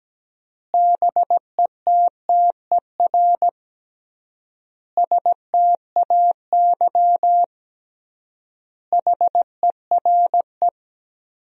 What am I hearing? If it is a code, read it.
Morse code, heard as a keyed tone, about 17 wpm, 700 Hz. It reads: BETTER STAY HERE
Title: 3x Word-Spacing - Sentences from Top 500 Words - 17wpm